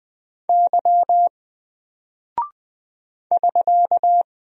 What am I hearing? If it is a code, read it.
Morse code, heard as a keyed tone, about 20 wpm, 700 Hz. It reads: Y <SK>
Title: Single Character (Letter:Number:Punctuation:Prosign) 20wpm